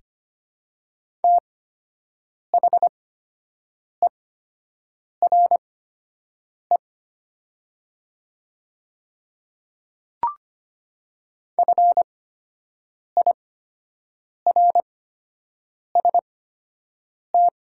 Morse code, heard as a keyed tone, about 25 wpm, spaced as for 7 wpm, 700 Hz. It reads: THERE FIRST
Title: 8x Character Spacing - Top 100 Words - 25wpm